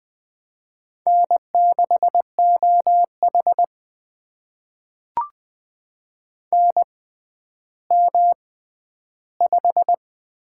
Morse code, heard as a keyed tone, about 20 wpm, spaced as for 11 wpm, 700 Hz. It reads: N6OH NM5
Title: USA Call Signs - 4 Characters - 6x Character Spacing - 20wpm